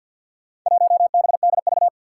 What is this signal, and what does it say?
Morse code, heard as a keyed tone, about 51 wpm, 700 Hz. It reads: 1BDV
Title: Four Challenging Letter-Number Combinations 50wpm